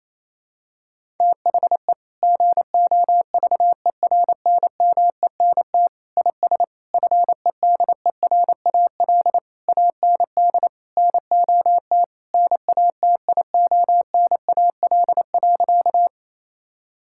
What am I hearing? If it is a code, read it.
Morse code, heard as a keyed tone, about 28 wpm, 700 Hz. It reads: THE GOVERNMENT IS FEDERAL AND NOT NATIONAL.
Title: Sentences from Top 1000 Words - Review Part 5 - 28wpm